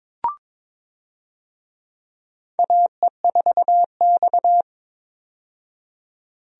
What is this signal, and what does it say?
Morse code, heard as a keyed tone, about 22 wpm, 700 Hz. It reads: AE4X
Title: USA Call Signs - 4 Characters - 22wpm